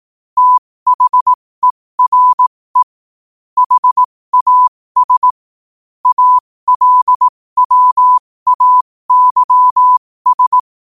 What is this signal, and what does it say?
Morse code, heard as a keyed tone, about 18 wpm, spaced as for 15 wpm, 1000 Hz. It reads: THERE HAS ALWAYS
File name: qotdfiles / 15wpm